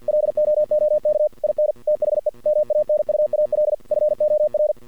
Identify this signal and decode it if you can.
Morse code, heard as a keyed tone, about 50 wpm, 610 Hz. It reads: LOOK IN THE MANUAL FOR
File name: archived / CRSnet